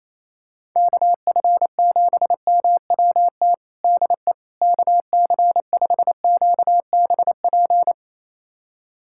Morse code, heard as a keyed tone, about 28 wpm, 700 Hz. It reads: KF7MWT DE KC5QBP
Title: USA Callsign DE Callsign - 6 Characters - 28wpm